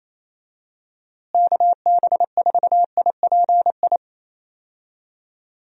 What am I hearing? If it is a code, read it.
Morse code, heard as a keyed tone, about 27 wpm, 700 Hz. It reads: KB4IPI
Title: USA Call Signs - 6 Characters - Repeated 2x - 28wpm